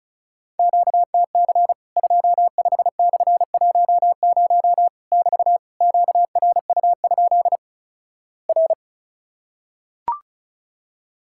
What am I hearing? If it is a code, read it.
Morse code, heard as a keyed tone, about 36 wpm, 700 Hz. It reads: QTC 25/10 <BT> QRU? R
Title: Worked All Europe DX Contest - QTC Exchange - 35wpm